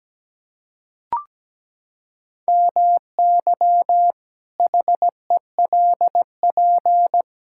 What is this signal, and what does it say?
Morse code, heard as a keyed tone, about 17 wpm, 700 Hz. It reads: MY HELP
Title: Pangram Challenge v2 - 17wpm